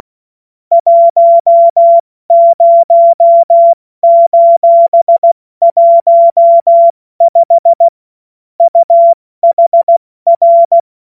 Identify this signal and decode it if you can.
Morse code, heard as a keyed tone, about 16 wpm, 680 Hz. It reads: 10:15 UHR
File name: Morsekurs / Witze / Witz 23. Dezember 2025 70 F80